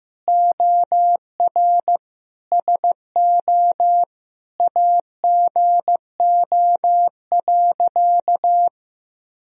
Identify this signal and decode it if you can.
Morse code, heard as a keyed tone, about 15 wpm, 700 Hz. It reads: OR SO AGO.
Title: Sentences from Top 1000 Words - Review Part 3 - 15wpm